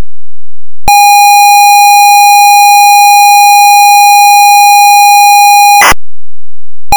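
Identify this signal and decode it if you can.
ZX Spectrum tape loading.
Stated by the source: ZX Spectrum game tape loading.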